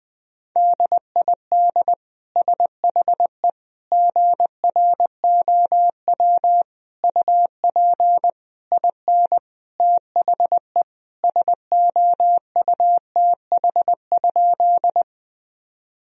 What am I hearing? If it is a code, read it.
Morse code, heard as a keyed tone, about 20 wpm, 700 Hz. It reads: DID SHE GROW UP IN THE SOUTH?
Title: Sentences from Top 1000 Words - Review Part 2 - 20wpm